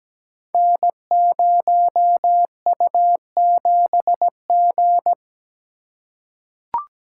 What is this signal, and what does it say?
Morse code, heard as a keyed tone, about 17 wpm, 700 Hz. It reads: N0U7G
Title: Code Groups - 5 Letter:Numbers - 17wpm